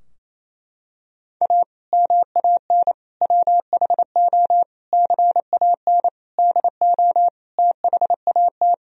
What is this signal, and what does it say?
Morse code, heard as a keyed tone, about 28 wpm, 700 Hz. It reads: A MAN WHO CAN DO THAT
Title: Sentences from Top 1000 Words - Review Part 2 - 28wpm